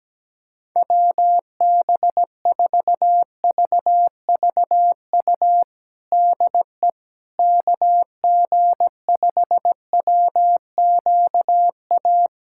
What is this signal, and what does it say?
Morse code, heard as a keyed tone, about 17 wpm, 700 Hz. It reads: WB4VVU DE KG5WQA
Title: USA Callsign DE Callsign - 6 Characters - 17wpm